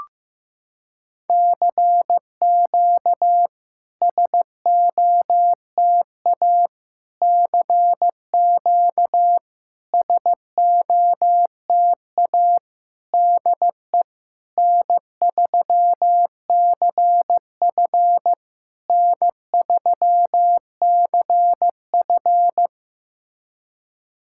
Morse code, heard as a keyed tone, about 15 wpm, 700 Hz. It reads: CQ SOTA CQ SOTA DE N3CF N3CF
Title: SOTA QSO Practice 15wpm